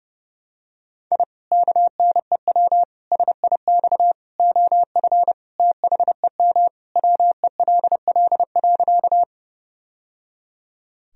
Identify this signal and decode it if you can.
Morse code, heard as a keyed tone, about 32 wpm, 700 Hz. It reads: I KNEW SIX OF THEM WELL.
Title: Sentences from Top 1000 Words - Review Part 2 - 30wpm